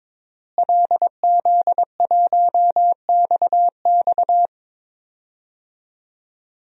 Morse code, heard as a keyed tone, about 22 wpm, 700 Hz. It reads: LZ1XX
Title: DXCC-100 Call Signs - 5 Characters - Repeated 2x - 22wpm